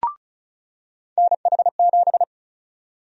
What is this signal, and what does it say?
Morse code, heard as a keyed tone, about 36 wpm, 700 Hz. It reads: NH7
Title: USA Call Sign Prefixes - 35wpm